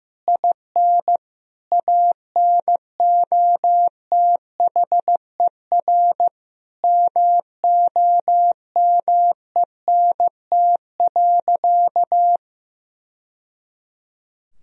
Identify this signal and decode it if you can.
Morse code, heard as a keyed tone, about 15 wpm, 700 Hz. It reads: IN ANOTHER MOMENT.
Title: Sentences from Top 1000 Words - Review Part 1 - 15wpm